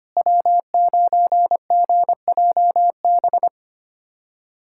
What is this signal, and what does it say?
Morse code, heard as a keyed tone, about 25 wpm, 700 Hz. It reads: W9GJB
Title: USA Callsign DE Callsign - 5 Characters - 25wpm